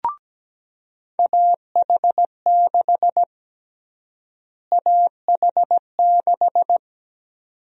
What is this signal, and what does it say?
Morse code, heard as a keyed tone, about 17 wpm, 700 Hz. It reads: AH6 AH6
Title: USA Call Sign Prefixes - Repeated 2x - 17wpm